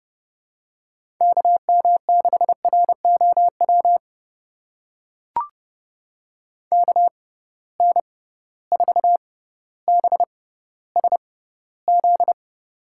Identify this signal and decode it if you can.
Morse code, heard as a keyed tone, about 30 wpm, spaced as for 15 wpm, 700 Hz. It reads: KM6ROW KN4BSZ
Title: USA Call Signs - 6 Characters - 6x Character Spacing - 30wpm